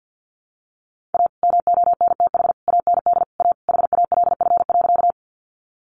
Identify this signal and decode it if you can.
Morse code, heard as a keyed tone, about 50 wpm, 700 Hz. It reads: A MONTH AND A HALF.
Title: Sentences from Top 800 Words 50wpm